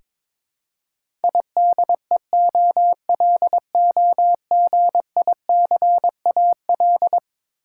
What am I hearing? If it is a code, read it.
Morse code, heard as a keyed tone, about 22 wpm, 700 Hz. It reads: IDEOLOGICAL
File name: practice / 22wpm / Eleven Letter Words - 22wpm